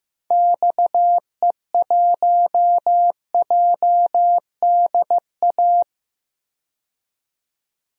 Morse code, heard as a keyed tone, about 16 wpm, 700 Hz. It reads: XE1JDA
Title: DXCC-100 Call Signs - 6 Characters - 15wpm